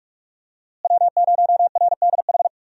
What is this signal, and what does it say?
Morse code, heard as a keyed tone, about 44 wpm, 700 Hz. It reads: W0RDH
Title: USA Call Signs - 5 Characters - Repeated 2x - 45wpm